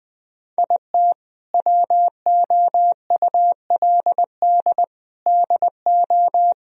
Morse code, heard as a keyed tone, about 20 wpm, 700 Hz. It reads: IT WOULD DO
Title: Sentences from Top 1000 Words - Review Part 3 - 20wpm